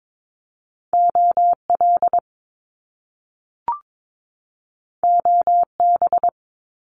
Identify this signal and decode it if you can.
Morse code, heard as a keyed tone, about 22 wpm, 700 Hz. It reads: OL OB
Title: Top Two Letter Combinations in Words 22wpm